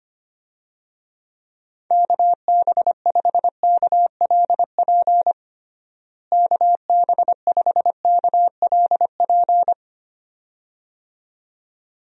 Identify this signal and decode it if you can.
Morse code, heard as a keyed tone, about 25 wpm, 700 Hz. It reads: KB5KLP KB5KLP
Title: USA Call Signs - 6 Characters - Repeated 2x - 25wpm